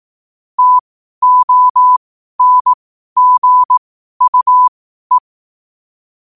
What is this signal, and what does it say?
Morse code, heard as a keyed tone, about 18 wpm, spaced as for 12 wpm, 1000 Hz. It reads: TONGUE
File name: qotdfiles / 12wpm